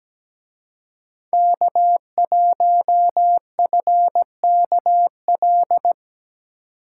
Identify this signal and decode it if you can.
Morse code, heard as a keyed tone, about 20 wpm, 700 Hz. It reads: K1FKL
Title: Speed Racing - USA 1x3 Call Signs 17wpm